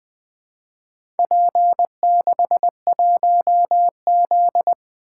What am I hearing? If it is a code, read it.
Morse code, heard as a keyed tone, about 20 wpm, 700 Hz. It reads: P61Z
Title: Four Letter-Number Combinations - 20wpm